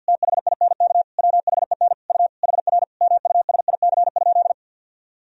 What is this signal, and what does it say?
Morse code, heard as a keyed tone, about 51 wpm, 700 Hz. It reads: THINK WHEN U HR MUSIC?
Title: Sentences from Top 300 Words and Abbreviations 50wpm